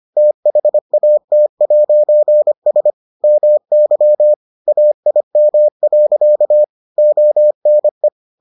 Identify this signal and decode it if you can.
Morse code, heard as a keyed tone, about 25 wpm, 600 Hz. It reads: THAT'S MY AIM. ONE